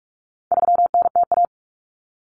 Morse code, heard as a keyed tone, about 44 wpm, 700 Hz. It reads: 3NTA
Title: Common Cut Numbers - 4-Digit 45wpm